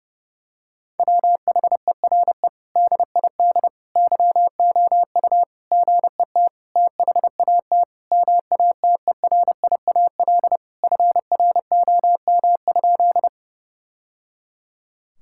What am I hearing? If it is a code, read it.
Morse code, heard as a keyed tone, about 32 wpm, 700 Hz. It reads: WHERE DID YOU GET THAT MATERIAL FROM?
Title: Sentences from Top 1000 Words - Review Part 5 - 30wpm